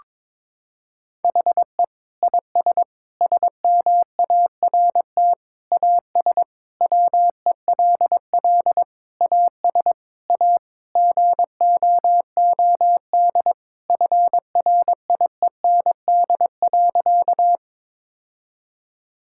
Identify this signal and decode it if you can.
Morse code, heard as a keyed tone, about 22 wpm, 700 Hz. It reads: HE IS SMART AS WELL AS A GOOD FRIEND.
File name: practice / 22wpm / Sentences with 4-Word Phrases - 22wpm